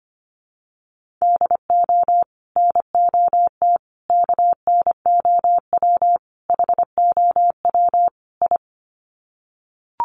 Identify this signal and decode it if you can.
Morse code, heard as a keyed tone, about 25 wpm, 700 Hz. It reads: DO NOT KNOW HOW I
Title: Sets of 5 Words Encore2 25wpm